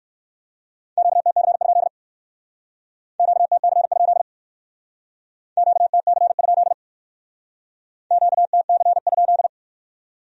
Morse code, heard as a keyed tone, about 58 wpm, 700 Hz. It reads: QTC? QTC? QTC? QTC?